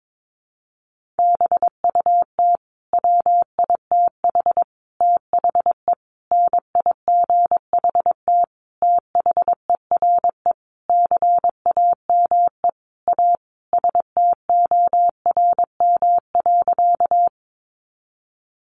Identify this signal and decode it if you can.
Morse code, heard as a keyed tone, about 22 wpm, 700 Hz. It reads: BUT WITH THE NIGHT THERE CAME A STORM.
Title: Sentences from Top 1500 Words 22wpm